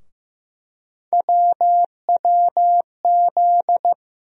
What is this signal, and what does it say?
Morse code, heard as a keyed tone, about 15 wpm, 700 Hz. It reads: WWZ
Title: USA Call Sign Postfixes - 3 Letters 15wpm